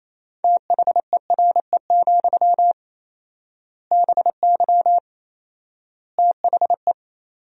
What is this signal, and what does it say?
Morse code, heard as a keyed tone, about 28 wpm, 700 Hz. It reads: THERE, BY THE
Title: 4x Word-Spacing - Sentences from Top 500 Words - 28wpm